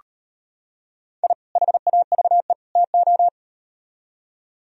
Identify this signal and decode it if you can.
Morse code, heard as a keyed tone, about 38 wpm, 700 Hz. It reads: I HAVE TO